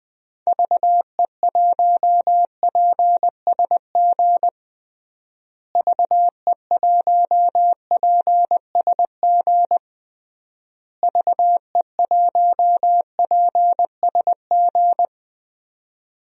Morse code, heard as a keyed tone, about 20 wpm, 700 Hz. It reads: VE1PSG VE1PSG VE1PSG
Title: DXCC-100 Call Signs - 6 Characters - Repeated 3x - 20wpm